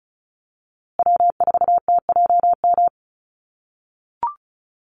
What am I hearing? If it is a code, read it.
Morse code, heard as a keyed tone, about 34 wpm, 700 Hz. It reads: W4TJM
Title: USA Call Signs Repeated 2x 35wpm